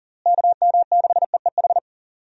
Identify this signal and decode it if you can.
Morse code, heard as a keyed tone, about 38 wpm, 700 Hz. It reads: KM6EEH
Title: USA Call Signs - 6 Characters - Repeated 2x - 40wpm